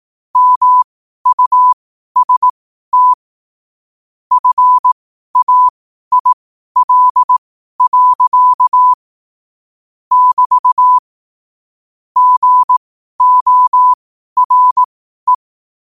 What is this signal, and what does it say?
Morse code, heard as a keyed tone, about 18 wpm, spaced as for 12 wpm, 1000 Hz. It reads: MUST FAIL. <BT> GORE
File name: qotdfiles / 12wpm